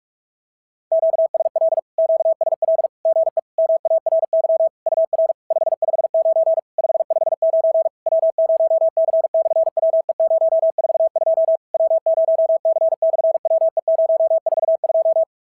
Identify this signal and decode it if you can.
Morse code, heard as a keyed tone, about 48 wpm, 650 Hz. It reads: QSL QSL GE MARY UR 559 559 W0C/WE042 W0C/WE042
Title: SOTA QSO Practice 45wpm